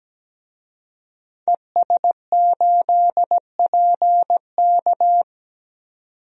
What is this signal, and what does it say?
Morse code, heard as a keyed tone, about 17 wpm, 700 Hz. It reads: ES8PK
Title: DXCC-100 Call Signs - 5 Characters - 17wpm